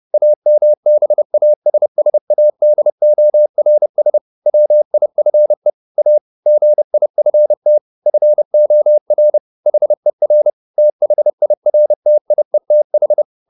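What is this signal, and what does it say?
Morse code, heard as a keyed tone, about 30 wpm, 600 Hz. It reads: AMBASSADORS WIFE A GIFT FOR HER THIRTIETH
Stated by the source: Morse code at 30 wpm, the novel Never by Ken Follett in English, from CW med Gnister.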